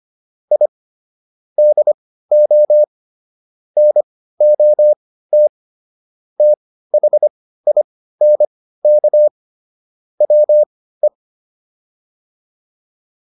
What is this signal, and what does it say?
Morse code, heard as a keyed tone, about 25 wpm, spaced as for 15 wpm, 600 Hz. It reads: I DO NOT THINK WE